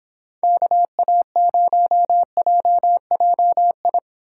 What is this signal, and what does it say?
Morse code, heard as a keyed tone, about 26 wpm, 700 Hz. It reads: KA0JJI